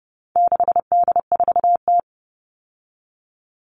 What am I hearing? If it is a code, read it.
Morse code, heard as a keyed tone, about 30 wpm, 700 Hz. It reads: 6D4T
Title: All Cut Numbers - 4-Digit 30wpm